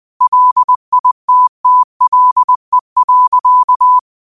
Morse code, heard as a keyed tone, about 20 wpm, 1000 Hz. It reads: LITTLE.